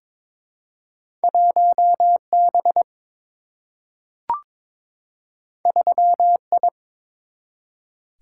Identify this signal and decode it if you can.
Morse code, heard as a keyed tone, about 22 wpm, 700 Hz. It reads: 1B 3I
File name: practice / 22wpm / Two Letter-Number Combinations - 22wpm